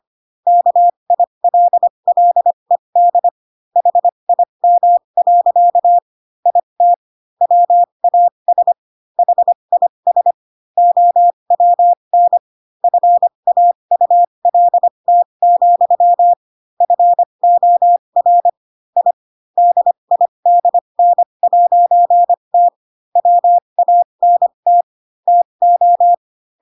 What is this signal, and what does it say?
Morse code, heard as a keyed tone, about 25 wpm, spaced as for 22 wpm, 700 Hz. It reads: KILLED HIM. IT WAS HIS OWN FAULT, FOR I DIDN'T WANT TO
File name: Pinocchio / Pinocchio ch7 18 25 ik1hge